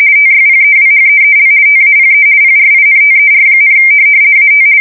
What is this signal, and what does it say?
RTTY, about 50 baud. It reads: TYPEWHEEL TAPE PRINTER, AT
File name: fsk170-50B / teletype-corp